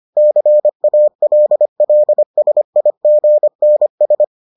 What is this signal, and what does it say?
Morse code, heard as a keyed tone, about 25 wpm, 600 Hz. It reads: CALLSIGNS